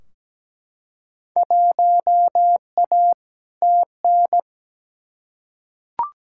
Morse code, heard as a keyed tone, about 17 wpm, 700 Hz. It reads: 1A TN
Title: Field Day Exchanges - 17wpm